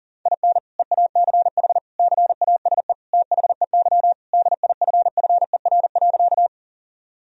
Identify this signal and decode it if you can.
Morse code, heard as a keyed tone, about 40 wpm, 700 Hz. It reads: IN EACH CASE THEY DIFFER.
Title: Sentences from Top 1000 Words - Review Part 1 - 40wpm